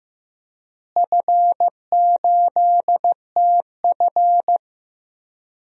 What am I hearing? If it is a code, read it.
Morse code, heard as a keyed tone, about 15 wpm, 700 Hz. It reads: F8TF
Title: DXCC-100 Call Signs - 4 Characters - Repeated 3x - 15wpm